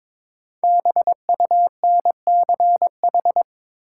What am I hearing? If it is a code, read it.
Morse code, heard as a keyed tone, about 22 wpm, 700 Hz. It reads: BUNCH
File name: 5 Letter Words - First 3 Letters Given - 22wpm